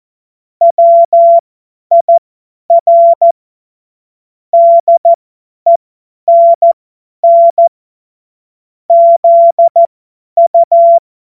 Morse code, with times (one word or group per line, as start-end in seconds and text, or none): 0.6-3.3 WIR
4.5-7.7 DENN
8.9-11.0 ZU